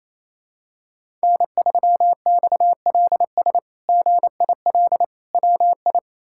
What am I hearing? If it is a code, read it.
Morse code, heard as a keyed tone, about 28 wpm, 700 Hz. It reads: N3XLS GIL WI
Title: K1USN SST QSO Practice - 28wpm